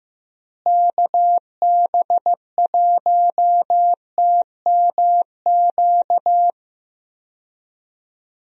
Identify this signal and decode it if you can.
Morse code, heard as a keyed tone, about 15 wpm, 700 Hz. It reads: KB1TMQ
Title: USA Call Signs - 6 Characters - Repeated 2x - 15wpm